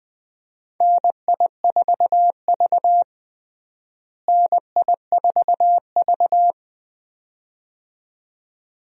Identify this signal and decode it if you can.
Morse code, heard as a keyed tone, about 21 wpm, 700 Hz. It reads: NI4V NI4V
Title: USA Call Signs - 4 Characters - Repeated 2x - 20wpm